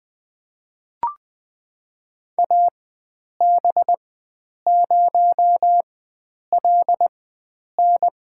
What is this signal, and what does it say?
Morse code, heard as a keyed tone, about 20 wpm, spaced as for 9 wpm, 700 Hz. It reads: AB0LN
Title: USA Call Signs - 5 Characters - 4x Character Spacing - 20wpm